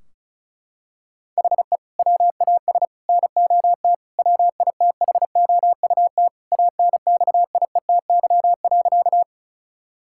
Morse code, heard as a keyed tone, about 34 wpm, 700 Hz. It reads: HE WAS NOT WITHOUT ANXIETY.
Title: Sentences from Top 1600 Words 35wpm